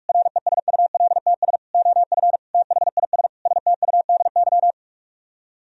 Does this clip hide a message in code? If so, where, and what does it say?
Yes: Morse code from 0.1 to 4.7 s, heard as a keyed tone, about 49 wpm, 700 Hz. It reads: RESULTS OF THIS STUDY